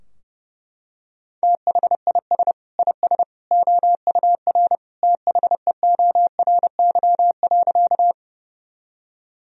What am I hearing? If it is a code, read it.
Morse code, heard as a keyed tone, about 30 wpm, 700 Hz. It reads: THIS IS OUR THEORY.
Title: Sentences from Top 1000 Words - Review Part 3 - 30wpm